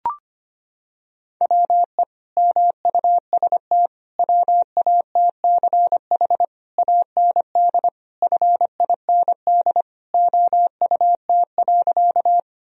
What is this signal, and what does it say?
Morse code, heard as a keyed tone, about 25 wpm, 700 Hz. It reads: WE MUST WATCH AND FIND OUT.
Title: Sentences from Top 1000 Words - Review Part 1 - 25wpm